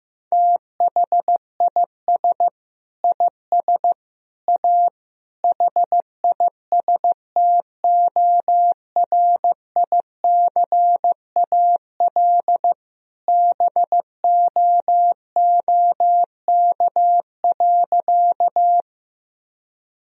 Morse code, heard as a keyed tone, about 15 wpm, 700 Hz. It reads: THIS IS A HISTORICAL BOOK.
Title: Sentences from Top 1000 Words - Review Part 4 - 15wpm